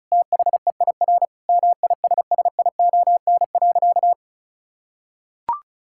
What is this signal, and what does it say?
Morse code, heard as a keyed tone, about 36 wpm, 700 Hz. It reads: THEIR MISSION.
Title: Sentences from Top 1600 Words 35wpm